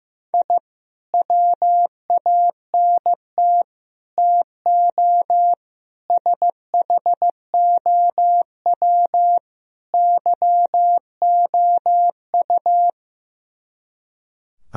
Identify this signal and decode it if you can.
Morse code, heard as a keyed tone, about 15 wpm, 690 Hz. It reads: I WANT TO SHOW YOU
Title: Speed-Racing - Sets of 5 Words 15wpm